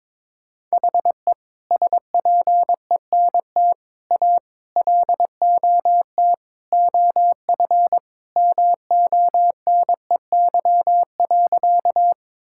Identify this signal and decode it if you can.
Morse code, heard as a keyed tone, about 22 wpm, 700 Hz. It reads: HE SPENT A LOT OF MONEY.
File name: practice / 22wpm / Sentences with 4-Word Phrases - 22wpm